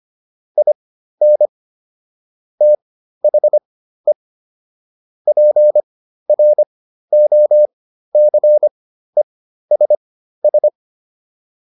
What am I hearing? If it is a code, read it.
Morse code, heard as a keyed tone, about 25 wpm, spaced as for 13 wpm, 600 Hz. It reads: IN THE PROCESS